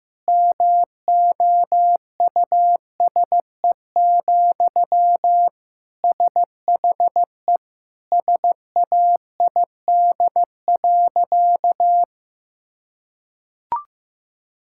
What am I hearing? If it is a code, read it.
Morse code, heard as a keyed tone, about 16 wpm, 700 Hz. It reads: MOUSE, SHE SAID.
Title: Sentences from Top 1000 Words - Review Part 5 - 15wpm